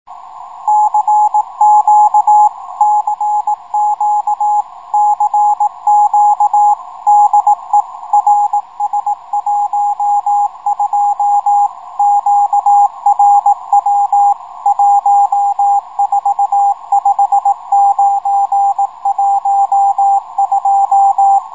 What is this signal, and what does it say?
Morse code, heard as a keyed tone, about 18 wpm, 780 Hz, with radio noise. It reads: CQ CQ CQ DE RS12 QRW 145912
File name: rs-12 robot